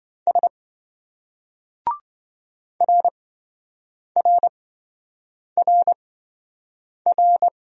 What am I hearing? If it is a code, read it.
Morse code, heard as a keyed tone, about 27 wpm, spaced as for 7 wpm, 700 Hz. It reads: S RRRR